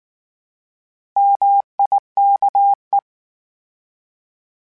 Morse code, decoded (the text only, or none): MIKE